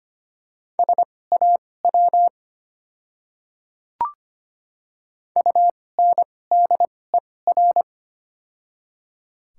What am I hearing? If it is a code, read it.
Morse code, heard as a keyed tone, about 25 wpm, spaced as for 18 wpm, 700 Hz. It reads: SAW UNDER
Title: 2x Character Spacing - Top 300 Words - 25wpm